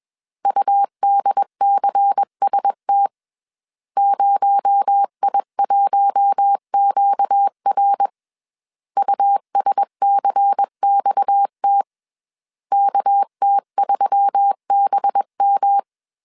Morse code, heard as a keyed tone, about 21 wpm, 800 Hz. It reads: UDCST 0I1QR USCXT KT3BM
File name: CW / DET / 1 6 1500C S A 20wpm 2048